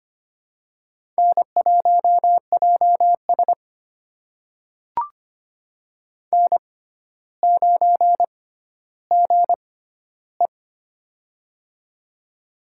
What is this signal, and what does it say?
Morse code, heard as a keyed tone, about 25 wpm, spaced as for 13 wpm, 700 Hz. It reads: N1JS N9GE
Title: USA Call Signs - 4 Characters - 6x Character Spacing - 25wpm